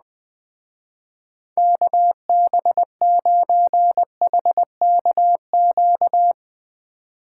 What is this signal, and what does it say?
Morse code, heard as a keyed tone, about 20 wpm, 700 Hz. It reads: KB9HKQ
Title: Speed Racing - USA 2x3 Call Signs 17wpm